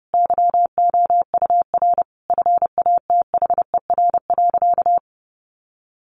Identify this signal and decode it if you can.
Morse code, heard as a keyed tone, about 30 wpm, 700 Hz. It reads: YOUR FATHER.
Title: Sentences from Top 1300 Words 30wpm